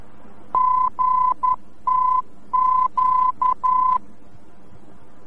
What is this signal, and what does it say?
Morse code, heard as a keyed tone, about 11 wpm, 1010 Hz. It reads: GTQ
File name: france / mx / GTQ-VOR-F